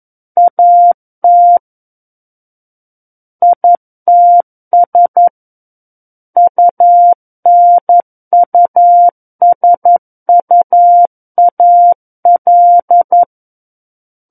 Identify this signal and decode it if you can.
Morse code, heard as a keyed tone, about 11 wpm, 700 Hz. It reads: AT ITS UNUSUAL
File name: 02-Chapter 2 - 11 WPM